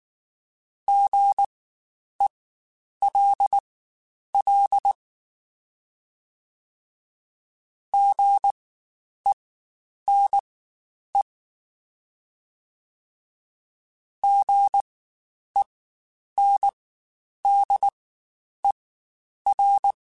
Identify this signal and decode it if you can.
Morse code, heard as a keyed tone, about 19 wpm, spaced as for 9 wpm, 800 Hz. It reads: GELL GENE GENDER
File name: Spaced Alphas / Spaced Alphas mp3s / Gs sp @18